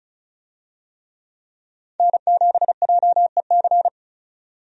Morse code, heard as a keyed tone, about 36 wpm, 700 Hz. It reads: N7JEC
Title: USA Call Signs - 5 Characters - 35wpm